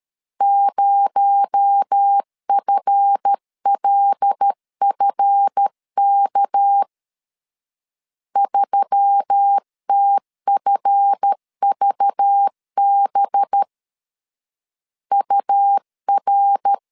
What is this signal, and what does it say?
Morse code, heard as a keyed tone, about 13 wpm, 800 Hz. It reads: 0FLFK 3TFVB UR
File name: LEZIONI / 7 GF9Z / 7 GF 12wpm 2122